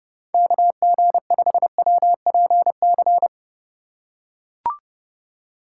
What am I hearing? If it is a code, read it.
Morse code, heard as a keyed tone, about 32 wpm, 700 Hz. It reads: KG5WPC
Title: USA Call Signs - 6 Characters - 2x Character Spacing - 30wpm